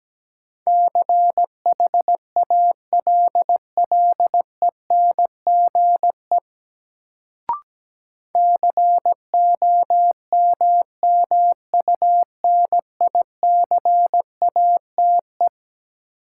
Morse code, heard as a keyed tone, about 17 wpm, 700 Hz. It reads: CHALLENGE COMMUNICATE
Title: Sentences from Top 1300 Words - New Word Drill Alt - 17wpm